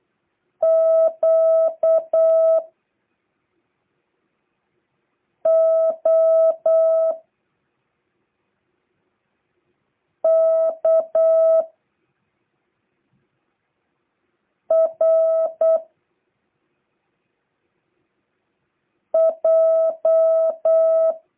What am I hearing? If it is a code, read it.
Morse code, heard as a keyed tone, about 8 wpm, 650 Hz. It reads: Q O K R J